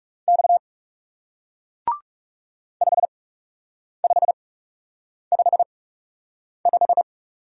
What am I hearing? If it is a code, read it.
Morse code, heard as a keyed tone, about 40 wpm, spaced as for 8 wpm, 700 Hz. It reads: X 5555